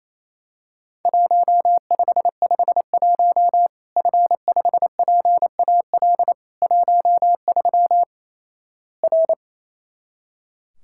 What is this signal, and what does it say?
Morse code, heard as a keyed tone, about 28 wpm, 700 Hz. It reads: 1551 F5PAL 13 R
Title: Worked All Europe DX Contest - QTC Exchange - 28wpm